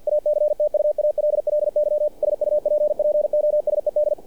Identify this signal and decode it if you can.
Morse code, heard as a keyed tone, about 48 wpm, 600 Hz, with radio noise. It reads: ACTUALLY SUPPOSED
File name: archived / CRSnet